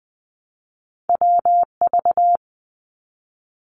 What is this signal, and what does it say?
Morse code, heard as a keyed tone, about 20 wpm, 700 Hz. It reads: WV
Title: US States 20wpm